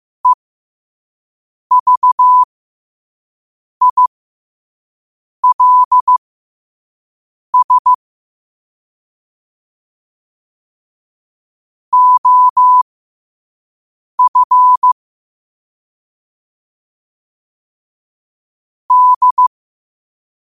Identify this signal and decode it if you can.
Morse code, heard as a keyed tone, about 15 wpm, spaced as for 5 wpm, 1000 Hz. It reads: EVILS OF D
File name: qotdfiles / 5wpm